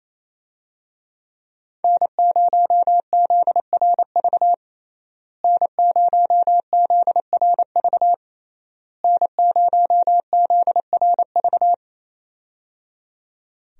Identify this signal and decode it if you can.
Morse code, heard as a keyed tone, about 28 wpm, 700 Hz. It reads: N0ZRV N0ZRV N0ZRV
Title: USA Call Signs - 5 Characters - Repeated 3x - 28wpm